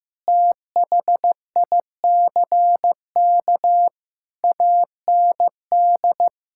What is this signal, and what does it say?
Morse code, heard as a keyed tone, about 16 wpm, 700 Hz. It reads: THICK AND
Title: Sentences from Top 1600 Words 15wpm